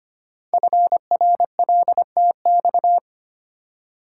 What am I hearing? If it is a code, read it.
Morse code, heard as a keyed tone, about 25 wpm, 700 Hz. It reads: FRLTX
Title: Code Groups - 5 Letters - 25wpm